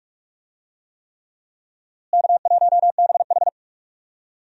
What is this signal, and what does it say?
Morse code, heard as a keyed tone, about 46 wpm, 700 Hz. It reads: K1BH
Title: USA Call Signs - 4 Characters - Repeated 3x - 45wpm